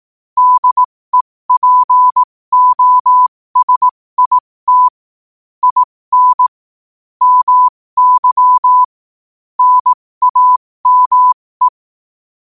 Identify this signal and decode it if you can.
Morse code, heard as a keyed tone, about 18 wpm, spaced as for 15 wpm, 1000 Hz. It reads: DEPOSIT IN MY NAME
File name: qotdfiles / 15wpm